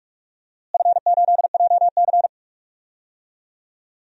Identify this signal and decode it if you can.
Morse code, heard as a keyed tone, about 45 wpm, 700 Hz. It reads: F8JC
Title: DXCC-100 Call Signs - 4 Characters - Repeated 2x - 45wpm